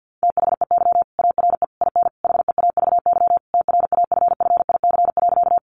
Morse code, heard as a keyed tone, about 50 wpm, 700 Hz. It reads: THEY ARE IN HEAVY TRAFFIC.
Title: Sentences from Top 1300 Words 50wpm